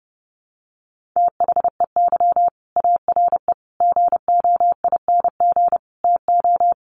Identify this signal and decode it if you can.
Morse code, heard as a keyed tone, about 30 wpm, 700 Hz. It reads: THEY ARE GOING TO
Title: Sets of 4 Words 30wpm